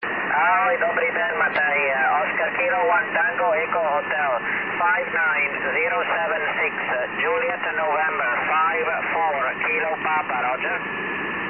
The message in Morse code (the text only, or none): none